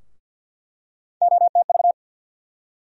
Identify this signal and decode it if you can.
Morse code, heard as a keyed tone, about 53 wpm, 700 Hz. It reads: OTV
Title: USA Call Sign Postfixes - 3 Letters 50wpm